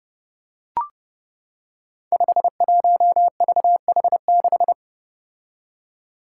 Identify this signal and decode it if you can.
Morse code, heard as a keyed tone, about 28 wpm, 700 Hz. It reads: 51VH6
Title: Five Challenging Letter-Number Combinations 30wpm